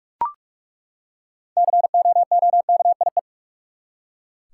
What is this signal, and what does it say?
Morse code, heard as a keyed tone, about 46 wpm, 700 Hz. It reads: COOKIE
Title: Six Letter Words - 45wpm